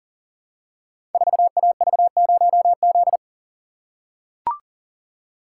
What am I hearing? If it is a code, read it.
Morse code, heard as a keyed tone, about 42 wpm, 690 Hz. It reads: 4AV0Z
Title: Code Groups - 5 Letter:Numbers - 40wpm